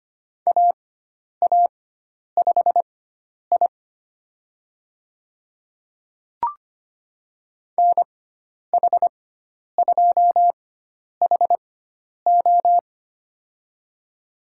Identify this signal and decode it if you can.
Morse code, heard as a keyed tone, about 25 wpm, spaced as for 10 wpm, 700 Hz. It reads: AA5I NH2HO
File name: Lesson 16g - Letter c - Call Signs 25wpm